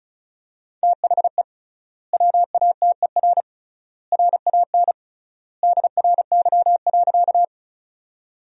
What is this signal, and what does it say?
Morse code, heard as a keyed tone, about 34 wpm, 700 Hz. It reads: THE WATER RAN DRY.
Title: 3x Word-Spacing - Sentences from Top 500 Words - 35wpm